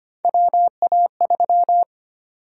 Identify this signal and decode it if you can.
Morse code, heard as a keyed tone, about 25 wpm, 700 Hz. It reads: WA3
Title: USA Call Sign Prefixes - 25wpm